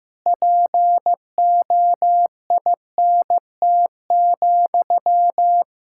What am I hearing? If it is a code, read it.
Morse code, heard as a keyed tone, about 15 wpm, 700 Hz. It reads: POINT,
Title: Sentences from Top 1600 Words 15wpm